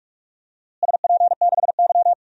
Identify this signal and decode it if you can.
Morse code, heard as a keyed tone, about 46 wpm, 700 Hz. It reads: SP6Y
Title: DXCC-100 Call Signs - 4 Characters - Repeated 3x - 45wpm